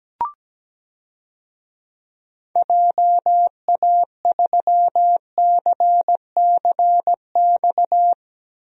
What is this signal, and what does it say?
Morse code, heard as a keyed tone, about 17 wpm, 700 Hz. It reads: JA3CCX
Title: DXCC-100 Call Signs - 6 Characters - Repeated 3x - 17wpm